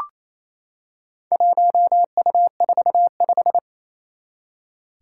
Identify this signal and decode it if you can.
Morse code, heard as a keyed tone, about 28 wpm, 700 Hz. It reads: 1U45
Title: Four Challenging Letter-Number Combinations 28wpm